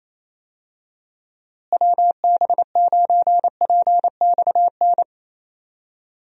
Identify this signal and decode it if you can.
Morse code, heard as a keyed tone, about 28 wpm, 700 Hz. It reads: WB9PXN
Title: USA Call Signs - 6 Characters - Repeated 3x - 28wpm